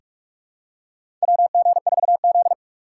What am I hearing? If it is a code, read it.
Morse code, heard as a keyed tone, about 45 wpm, 700 Hz. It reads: WG4Z
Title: USA Call Signs - 4 Characters - 6x Character Spacing - 45wpm